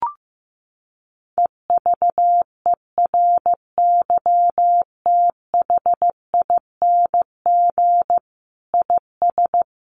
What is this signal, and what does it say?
Morse code, heard as a keyed tone, about 15 wpm, 700 Hz. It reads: EVERYTHING IS
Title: Sentences from Top 700 Words 15wpm